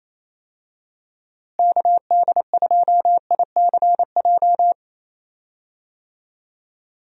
Morse code, heard as a keyed tone, about 28 wpm, 700 Hz. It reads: KD2ICJ
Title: USA Call Signs - 6 Characters - Repeated 3x - 28wpm